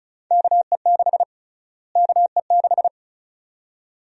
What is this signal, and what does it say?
Morse code, heard as a keyed tone, about 36 wpm, 700 Hz. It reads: KE6 KE6
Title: USA Call Sign Prefixes - Repeated 2x - 35wpm